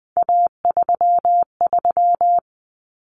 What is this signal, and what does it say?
Morse code, heard as a keyed tone, about 20 wpm, 700 Hz. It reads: A33
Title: Common Cut Numbers - 3-Digit 20wpm